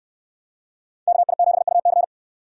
Morse code, heard as a keyed tone, about 69 wpm, 700 Hz. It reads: KI7UC